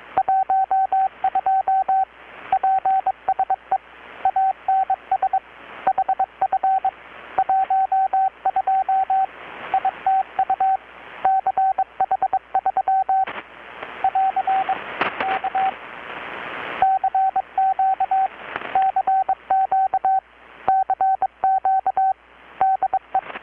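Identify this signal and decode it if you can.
Morse code, heard as a keyed tone, about 23 wpm, 770 Hz, with radio noise. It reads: 12 PSE ANS HF 12 ITU CH3 <AR> K CQ CQ CQ DE
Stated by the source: Morse from KPH's Night of Nights XX maritime radiotelegraphy event, received in Chiapas, Mexico about 3,800 km away on a Kenwood R-600 with a grounded 10 m long-wire antenna.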